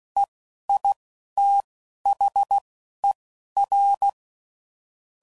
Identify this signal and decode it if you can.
Morse code, heard as a keyed tone, about 16 wpm, spaced as for 11 wpm, 800 Hz. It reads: EITHER
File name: CW Code Groups @10WPM / Technical Subjects / AND Gate